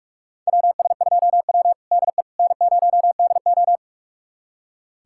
Morse code, heard as a keyed tone, about 44 wpm, 700 Hz. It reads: WS2W DE N0DO
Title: USA Callsign DE Callsign - 4 Characters - 45wpm